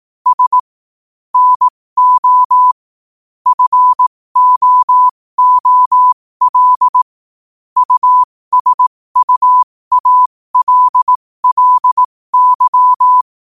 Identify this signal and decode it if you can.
Morse code, heard as a keyed tone, about 18 wpm, spaced as for 15 wpm, 1000 Hz. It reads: S NO FOOL USUALLY
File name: qotdfiles / 15wpm